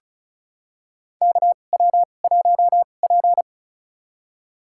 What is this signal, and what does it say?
Morse code, heard as a keyed tone, about 36 wpm, spaced as for 25 wpm, 700 Hz. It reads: KW1P
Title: USA Call Signs - 2x Character Spacing - 35wpm